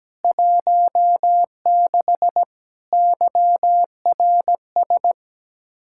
Morse code, heard as a keyed tone, about 17 wpm, 700 Hz. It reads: 16 YRS
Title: Practice QSO - Exchange 3 - More Info - 17wpm